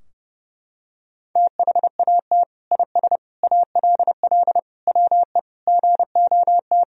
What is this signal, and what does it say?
Morse code, heard as a keyed tone, about 30 wpm, 700 Hz. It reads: THAT IS ALL WE GOT
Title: Sentences from Top 1000 Words - Review Part 2 - 30wpm